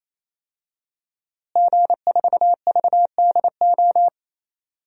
Morse code, heard as a keyed tone, about 28 wpm, 700 Hz. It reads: G4VDO
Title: DXCC-100 Call Signs - 5 Characters - Repeated 3x - 28wpm